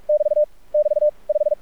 Morse code, heard as a keyed tone, about 46 wpm, 600 Hz. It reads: <BT> <BT> 5
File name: archived / CRSnet